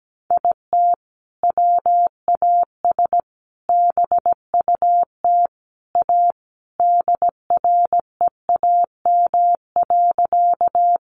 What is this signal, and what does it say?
Morse code, heard as a keyed tone, about 17 wpm, 700 Hz. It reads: IT WAS BUT A DREAM.
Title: Sentences from Top 1500 Words 17wpm